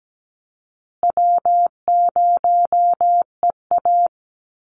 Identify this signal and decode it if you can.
Morse code, heard as a keyed tone, about 17 wpm, 700 Hz. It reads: W0EA
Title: Easy Call Signs Repeated 2x - 17wpm